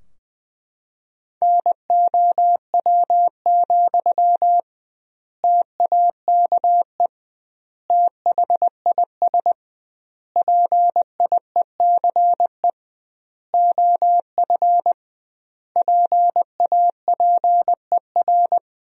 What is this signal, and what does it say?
Morse code, heard as a keyed tone, about 20 wpm, 700 Hz. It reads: NOW, TAKE THIS PIECE OF PAPER
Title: 2x Word-Spacing - Sentences from Top 400 Words - 20wpm